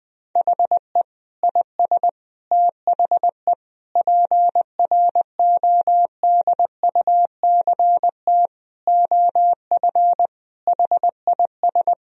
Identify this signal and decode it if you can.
Morse code, heard as a keyed tone, about 20 wpm, 700 Hz. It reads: HE IS THE PRODUCT OF HIS
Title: Sentences from Top 1000 Words - Review Part 2 - 20wpm